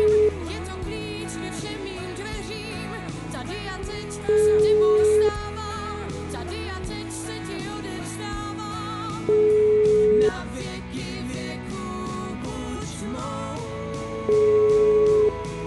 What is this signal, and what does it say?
Ringback tone (Europe, 425 Hz).